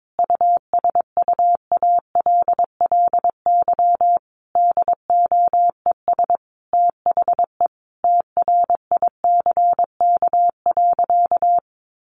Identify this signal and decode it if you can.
Morse code, heard as a keyed tone, about 22 wpm, 700 Hz. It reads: USUALLY DOES THE TRICK.
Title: Sentences from Top 1500 Words 22wpm